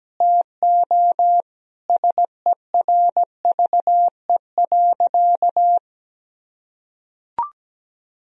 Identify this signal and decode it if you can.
Morse code, heard as a keyed tone, about 17 wpm, 700 Hz. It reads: TO SERVE.
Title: Sentences from Top 1000 Words - Review Part 4 - 17wpm